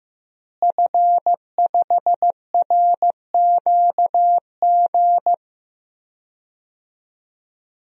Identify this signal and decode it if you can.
Morse code, heard as a keyed tone, about 15 wpm, 700 Hz. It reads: F5RQG
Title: DXCC-100 Call Signs - 5 Characters - Repeated 3x - 15wpm